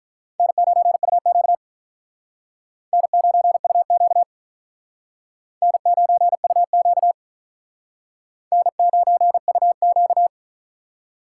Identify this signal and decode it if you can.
Morse code, heard as a keyed tone, about 45 wpm, 700 Hz. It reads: N9UQ N9UQ N9UQ N9UQ